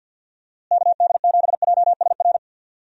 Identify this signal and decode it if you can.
Morse code, heard as a keyed tone, about 51 wpm, 700 Hz. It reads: KD7JSR
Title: USA Call Signs - 6 Characters - Repeated 3x - 50wpm